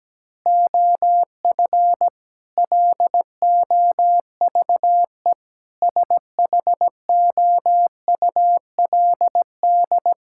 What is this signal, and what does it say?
Morse code, heard as a keyed tone, about 17 wpm, 700 Hz. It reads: OF LOVE SHOULD
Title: Sentences from Top 1600 Words 17wpm